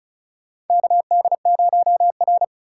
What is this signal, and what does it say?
Morse code, heard as a keyed tone, about 36 wpm, 700 Hz. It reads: KD0R
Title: USA Call Signs - 4 Characters - 4x Character Spacing - 35wpm